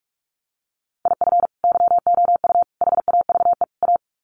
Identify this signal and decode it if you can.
Morse code, heard as a keyed tone, about 44 wpm, 700 Hz. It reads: IF YOU HAVE A
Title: Sets of 4 Words 45wpm